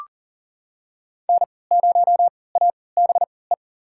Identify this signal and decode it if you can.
Morse code, heard as a keyed tone, about 40 wpm, spaced as for 23 wpm, 700 Hz. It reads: N0ABE
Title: USA Call Signs - 5 Characters - 3x Character Spacing - 40wpm